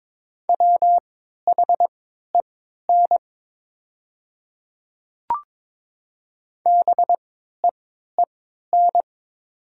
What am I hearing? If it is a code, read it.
Morse code, heard as a keyed tone, about 22 wpm, spaced as for 12 wpm, 700 Hz. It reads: WHEN BEEN
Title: 3x Character Spacing - Top 100 Words - 22wpm